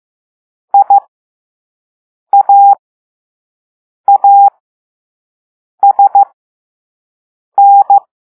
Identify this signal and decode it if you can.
Morse code, heard as a keyed tone, about 15 wpm, spaced as for 5 wpm, 800 Hz. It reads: IAASN
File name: Morse / Level3-All-B